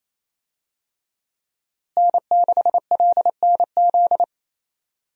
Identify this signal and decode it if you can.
Morse code, heard as a keyed tone, about 28 wpm, 700 Hz. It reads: N6LNZ
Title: USA Call Signs - 5 Characters - Repeated 2x - 28wpm